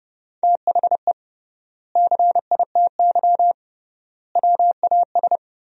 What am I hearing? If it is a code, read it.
Morse code, heard as a keyed tone, about 30 wpm, 700 Hz. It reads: THE CITY WAS
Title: 3x Word-Spacing - Sentences from Top 500 Words - 30wpm